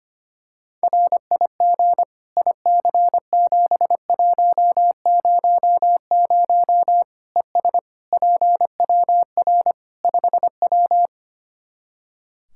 Morse code, heard as a keyed tone, about 25 wpm, 700 Hz. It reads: RIG IC7100 ES PWR 5W
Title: Practice QSO - Exchange 2 - Info - 25wpm